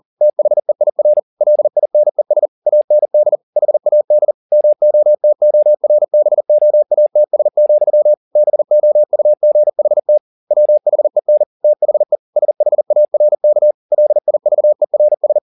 Morse code, heard as a keyed tone, about 40 wpm, 600 Hz. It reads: THEIR LINES AND HAD MOTORBOATS, BOUGHT WHEN THE SHARK LIVERS